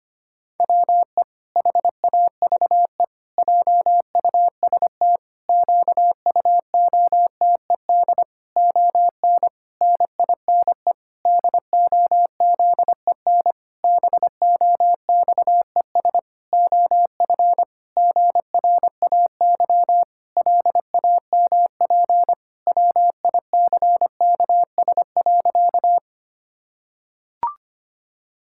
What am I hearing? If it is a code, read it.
Morse code, heard as a keyed tone, about 25 wpm, 700 Hz. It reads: WE HAVE JUST QUOTED ON NINE DOZEN BOXES OF GRAY LAMP WICKS.
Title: Pangram Challenge v2 - 25wpm